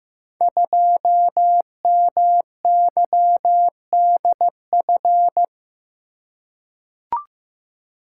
Morse code, heard as a keyed tone, about 15 wpm, 700 Hz. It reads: 2MYDF
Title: Code Groups - 5 Letter:Numbers - 15wpm